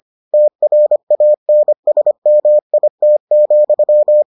Morse code, heard as a keyed tone, about 25 wpm, 600 Hz. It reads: TRANSMIT,